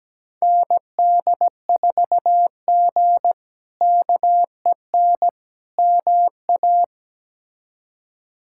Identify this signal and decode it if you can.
Morse code, heard as a keyed tone, about 17 wpm, 700 Hz. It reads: ND4G KEN MA
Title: K1USN SST QSO Practice - 17wpm